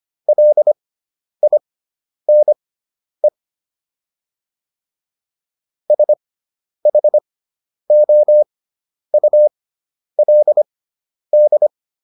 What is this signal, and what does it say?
Morse code, heard as a keyed tone, about 25 wpm, spaced as for 10 wpm, 600 Hz. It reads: LINE SHOULD